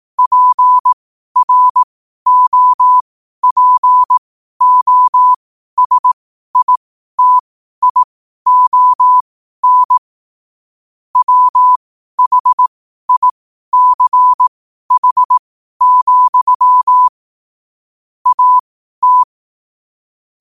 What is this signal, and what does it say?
Morse code, heard as a keyed tone, about 18 wpm, spaced as for 12 wpm, 1000 Hz. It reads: PROPOSITION WHICH, AT
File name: qotdfiles / 12wpm